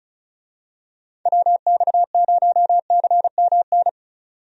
Morse code, heard as a keyed tone, about 36 wpm, 700 Hz. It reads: WX0CMN
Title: USA Call Signs - 6 Characters - 35wpm